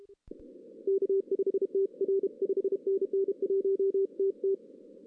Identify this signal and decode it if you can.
Morse code, heard as a keyed tone, about 32 wpm, 380 Hz. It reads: K5TR 5NN 1TT
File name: 2016 arrl dx cw / 2016-ARRL-DX-CW-K5TR- - 0-3